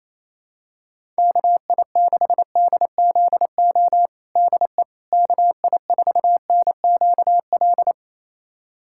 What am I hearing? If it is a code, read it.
Morse code, heard as a keyed tone, about 28 wpm, 700 Hz. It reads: KI6DZO DE KI4NQL
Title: USA Callsign DE Callsign 28wpm